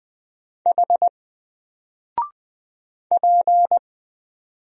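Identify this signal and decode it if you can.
Morse code, heard as a keyed tone, about 20 wpm, 700 Hz. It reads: H P
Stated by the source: Single letter and number Morse drill at 20 wpm, from N4LCD.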